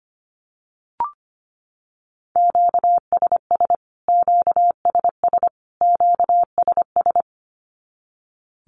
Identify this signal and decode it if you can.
Morse code, heard as a keyed tone, about 25 wpm, 700 Hz. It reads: QSS QSS QSS
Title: Q-Codes Repeated 3x 25wpm